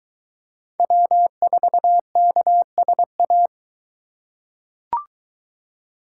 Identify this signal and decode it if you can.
Morse code, heard as a keyed tone, about 23 wpm, 700 Hz. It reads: W4KSA